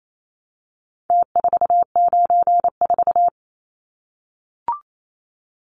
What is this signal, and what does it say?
Morse code, heard as a keyed tone, about 28 wpm, 700 Hz. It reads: T494
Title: Numbers - Contest Style - 4-Digit 28wpm